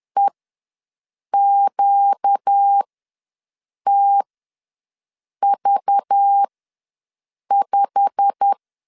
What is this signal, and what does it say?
Morse code, heard as a keyed tone, about 10 wpm, spaced as for 6 wpm, 800 Hz. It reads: EQTV5
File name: LEZIONI / 9 2867 ws 10wpm 4215